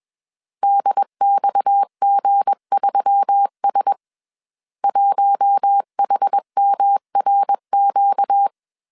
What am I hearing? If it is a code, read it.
Morse code, heard as a keyed tone, about 21 wpm, 800 Hz. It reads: DXG3S 1HMRQ
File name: CW / DET / 1 9 1500C S C 20wpm 2310